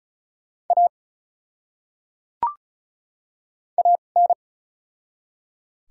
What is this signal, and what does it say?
Morse code, heard as a keyed tone, about 34 wpm, spaced as for 25 wpm, 700 Hz. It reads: A AN
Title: 2x Character Spacing - Top 100 Words - 35wpm